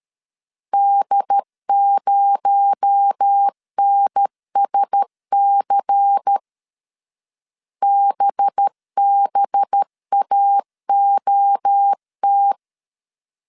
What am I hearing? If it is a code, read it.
Morse code, heard as a keyed tone, about 13 wpm, 800 Hz. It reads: D0NSC BBAOT
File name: CW / DET / 1 6 1500C S B 12wpm 3431